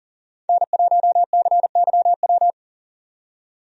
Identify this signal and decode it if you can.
Morse code, heard as a keyed tone, about 40 wpm, 700 Hz. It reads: N1CYW
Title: USA Call Signs - 5 Characters - 40wpm